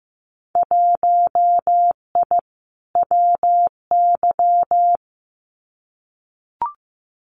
Morse code, heard as a keyed tone, about 15 wpm, 700 Hz. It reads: 1I WY
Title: Winter Field Day Exchanges Encore 15wpm